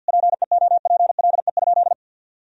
Morse code, heard as a keyed tone, about 52 wpm, 700 Hz. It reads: PEOPLE?
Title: Sentences from Top 1000 Words - Review Part 3 - 50wpm